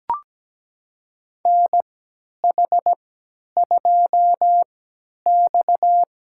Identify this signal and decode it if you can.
Morse code, heard as a keyed tone, about 17 wpm, spaced as for 10 wpm, 700 Hz. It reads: NH2X
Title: USA Call Signs - 4 Characters - 3x Character Spacing - 17wpm